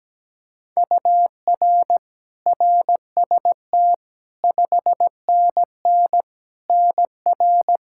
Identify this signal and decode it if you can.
Morse code, heard as a keyed tone, about 17 wpm, 700 Hz. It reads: UR RST 5NN NR
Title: Practice QSO - Exchange 1 - Beginning - 17wpm